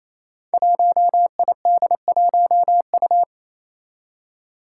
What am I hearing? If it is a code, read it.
Morse code, heard as a keyed tone, about 28 wpm, 700 Hz. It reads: 1ID1U
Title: Five Challenging Letter-Number Combinations 28wpm